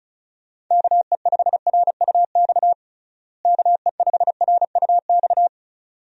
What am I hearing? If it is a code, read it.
Morse code, heard as a keyed tone, about 36 wpm, 700 Hz. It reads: KE5RUX KE5RUX
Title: USA Call Signs - 6 Characters - Repeated 2x - 35wpm